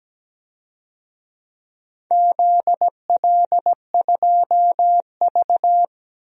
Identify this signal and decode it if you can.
Morse code, heard as a keyed tone, about 17 wpm, 700 Hz. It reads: ZL2V
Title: DXCC-100 Call Signs - 4 Characters - 17wpm